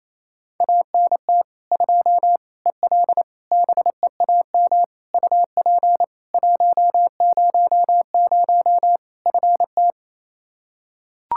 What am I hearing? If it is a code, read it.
Morse code, heard as a keyed tone, about 28 wpm, 700 Hz. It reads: ANT 2 EL BEAM UP 100 FT
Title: Practice QSO - Exchange 2 - Info - 28wpm